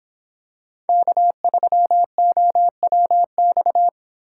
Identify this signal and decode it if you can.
Morse code, heard as a keyed tone, about 25 wpm, 700 Hz. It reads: K3OWX